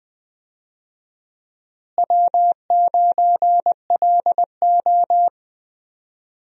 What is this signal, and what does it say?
Morse code, heard as a keyed tone, about 20 wpm, 700 Hz. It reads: W9LO
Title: USA Call Signs - 4 Characters - Repeated 2x - 20wpm